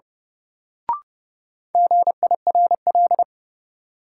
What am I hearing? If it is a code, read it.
Morse code, heard as a keyed tone, about 32 wpm, 700 Hz. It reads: GIRL
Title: Sentences from Top 400 Words - New Word Drill - 30wpm